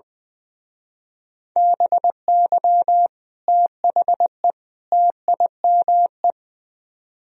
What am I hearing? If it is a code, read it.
Morse code, heard as a keyed tone, about 21 wpm, 700 Hz. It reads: BY THE TIME
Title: Speed-Racing - Sets of 3 Words 20wpm